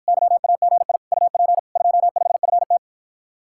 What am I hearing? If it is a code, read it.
Morse code, heard as a keyed tone, about 54 wpm, 700 Hz. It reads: YAGI UP 25FT